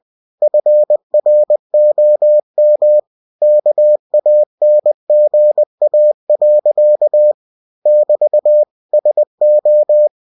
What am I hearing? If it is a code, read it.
Morse code, heard as a keyed tone, about 20 wpm, 600 Hz. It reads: FROM KANGA. <BT> SO
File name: cw-books / winnie-the-pooh / winnie-20wpm-page-0085